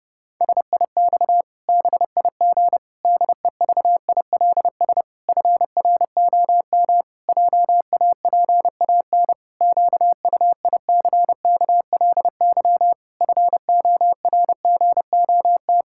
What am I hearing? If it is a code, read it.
Morse code, heard as a keyed tone, about 32 wpm, 700 Hz. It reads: SIX BIG DEVILS FROM JAPAN QUICKLY FORGOT
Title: Pangram Challenge v2 - 30wpm